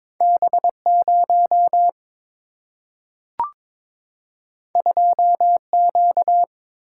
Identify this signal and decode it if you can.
Morse code, heard as a keyed tone, about 22 wpm, 700 Hz. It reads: B0 2Q
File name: practice / 22wpm / Two Letter-Number Combinations - 22wpm